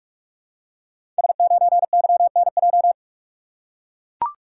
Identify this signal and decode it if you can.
Morse code, heard as a keyed tone, about 44 wpm, 700 Hz. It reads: S9YNJ
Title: Code Groups - 5 Letter:Numbers - 45wpm